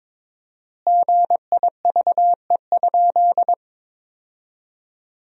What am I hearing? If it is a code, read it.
Morse code, heard as a keyed tone, about 22 wpm, 700 Hz. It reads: GIVE?
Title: 4x Word-Spacing - Sentences from Top 200 Words - 22wpm